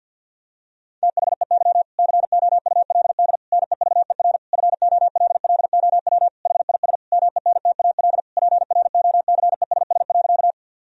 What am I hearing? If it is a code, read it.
Morse code, heard as a keyed tone, about 50 wpm, 700 Hz. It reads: THEY COULD NEVER FOLLOW HIS MENTAL PROCESS.
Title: Sentences from Top 1000 Words - Review Part 4 - 50wpm